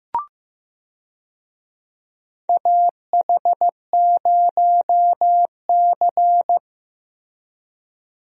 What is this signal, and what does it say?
Morse code, heard as a keyed tone, about 16 wpm, 700 Hz. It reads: AH0C
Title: USA Call Signs - 4 Characters - Repeated 2x - 15wpm